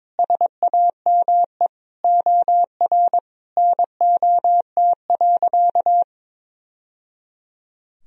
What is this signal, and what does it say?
Morse code, heard as a keyed tone, about 22 wpm, 700 Hz. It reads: SAME OR NOT.
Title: Sentences from Top 1000 Words - Review Part 4 - 22wpm